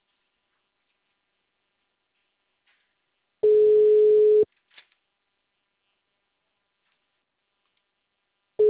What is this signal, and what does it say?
Ringback tone (Europe, 425 Hz).
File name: out- -2012- - - .5247